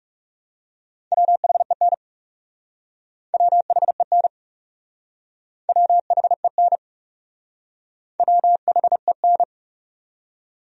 Morse code, heard as a keyed tone, about 38 wpm, 700 Hz. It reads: WHEN WHEN WHEN WHEN
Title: Speed-Racing Top 100 Words 30wpm